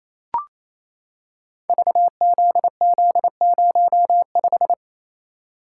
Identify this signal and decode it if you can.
Morse code, heard as a keyed tone, about 28 wpm, 700 Hz. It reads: VZZ05
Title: Code Groups - 5 Letter:Numbers - 28wpm